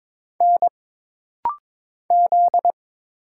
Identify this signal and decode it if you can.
Morse code, heard as a keyed tone, about 22 wpm, 700 Hz. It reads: N Z
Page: This practice set is composed of Letters, Numbers, common punctuations, and common prosigns.